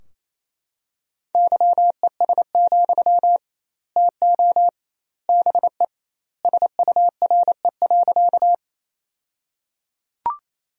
Morse code, heard as a keyed tone, about 28 wpm, 700 Hz. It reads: YES, TO BE SURE.
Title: 2x Word-Spacing - Sentences from Top 500 Words - 28wpm